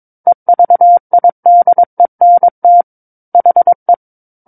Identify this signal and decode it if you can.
Morse code, heard as a keyed tone, about 22 wpm, 700 Hz. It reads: EVIDENT HE
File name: Book One / 13-Chapter 13 - 22 WPM